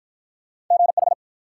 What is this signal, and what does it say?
Morse code, heard as a keyed tone, about 45 wpm, 700 Hz. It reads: GH